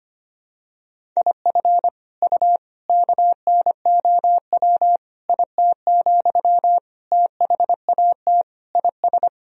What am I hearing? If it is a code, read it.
Morse code, heard as a keyed tone, about 25 wpm, 700 Hz. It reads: IF U KNOW IT, THAT IS
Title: Sentences from Top 400 Words and Abbreviations 25wpm